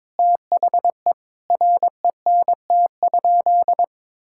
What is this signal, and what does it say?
Morse code, heard as a keyed tone, about 22 wpm, 700 Hz. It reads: THE RENT?
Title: Sentences from Top 1000 Words - Review Part 5 - 22wpm